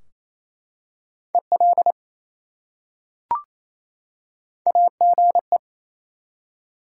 Morse code, heard as a keyed tone, about 28 wpm, 700 Hz. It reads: EL AGE
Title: Top 180 QSO Elements 28wpm